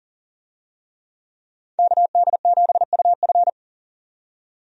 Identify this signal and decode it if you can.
Morse code, heard as a keyed tone, about 40 wpm, 700 Hz. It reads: KD7UF
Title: USA Call Signs - 5 Characters - 40wpm